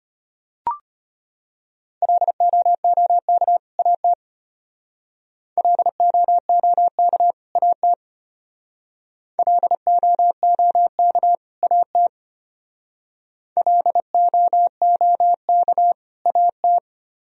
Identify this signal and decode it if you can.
Morse code, heard as a keyed tone, about 32 wpm, 700 Hz. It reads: LOOK AT LOOK AT LOOK AT LOOK AT